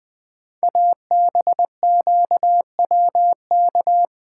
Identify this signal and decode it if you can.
Morse code, heard as a keyed tone, about 20 wpm, 700 Hz. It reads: ABQWK
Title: Code Groups - 5 Letters - 20wpm